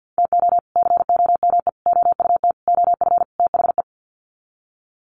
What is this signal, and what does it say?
Morse code, heard as a keyed tone, about 50 wpm, 700 Hz. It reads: TO COME OUT OF THE
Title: Sets of 5 Words Encore2 50wpm